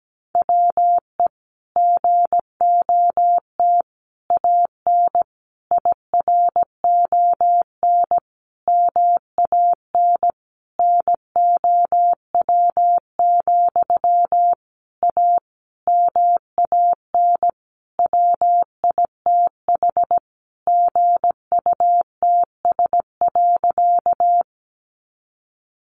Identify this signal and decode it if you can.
Morse code, heard as a keyed tone, about 17 wpm, 700 Hz. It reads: WE GOT AN IRON MAN NOW, A MAN WITH GUTS.
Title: A Sound Of Thunder 17wpm